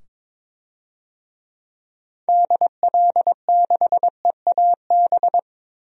Morse code, heard as a keyed tone, about 22 wpm, 700 Hz. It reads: DL6EAB
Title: DXCC-100 Call Signs - 6 Characters - 22wpm